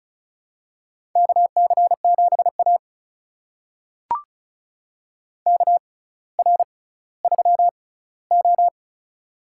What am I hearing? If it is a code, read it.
Morse code, heard as a keyed tone, about 36 wpm, spaced as for 18 wpm, 700 Hz. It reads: KC7A KR3O
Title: USA Call Signs - 4 Characters - 6x Character Spacing - 35wpm